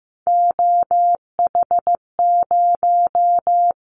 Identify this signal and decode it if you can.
Morse code, heard as a keyed tone, about 15 wpm, 700 Hz. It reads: OH0
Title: Callsign Prefixes for Top 100 DXCC - 15wpm